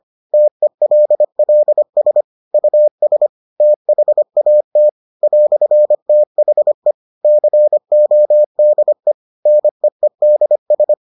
Morse code, heard as a keyed tone, about 25 wpm, 600 Hz. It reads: TELLS US THAT "THE CODE NEEDS